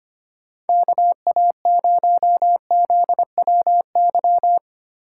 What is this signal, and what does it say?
Morse code, heard as a keyed tone, about 31 wpm, 700 Hz. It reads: KA0ZWY
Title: Speed Racing - USA Call Signs 25wpm